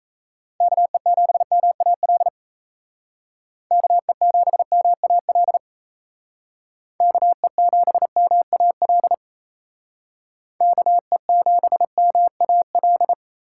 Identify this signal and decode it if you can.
Morse code, heard as a keyed tone, about 33 wpm, 700 Hz. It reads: KE7MAL KE7MAL KE7MAL KE7MAL
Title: Speed Racing - USA 2x3 Call Signs 28wpm